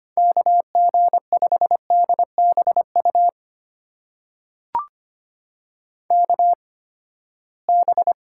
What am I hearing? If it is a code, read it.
Morse code, heard as a keyed tone, about 25 wpm, spaced as for 17 wpm, 700 Hz. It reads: KG5DBU KB
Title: USA Call Signs - 6 Characters - 8x Character Spacing - 25wpm